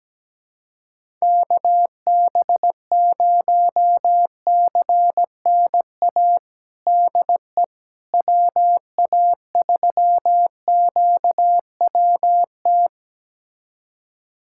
Morse code, heard as a keyed tone, about 17 wpm, 700 Hz. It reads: KB0CNA DE WA3QWT
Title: USA Callsign DE Callsign - 6 Characters - 17wpm